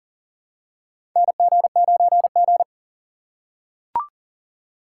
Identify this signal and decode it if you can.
Morse code, heard as a keyed tone, about 40 wpm, 690 Hz. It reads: NG9G
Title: USA Call Signs - 2x Character Spacing - 40wpm